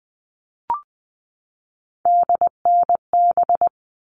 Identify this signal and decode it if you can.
Morse code, heard as a keyed tone, about 20 wpm, 700 Hz. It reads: DNB
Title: All Cut Numbers - 3-Digit 20wpm